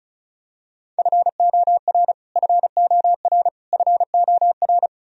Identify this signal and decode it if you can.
Morse code, heard as a keyed tone, about 36 wpm, 700 Hz. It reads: FOR FOR FOR
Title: Top 100 Words Repeated 35wpm